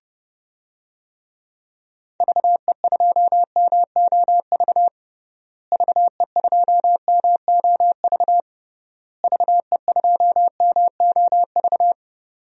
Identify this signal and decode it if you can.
Morse code, heard as a keyed tone, about 32 wpm, 700 Hz. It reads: VE2MOV VE2MOV VE2MOV
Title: DXCC-100 Call Signs - 6 Characters - Repeated 3x - 30wpm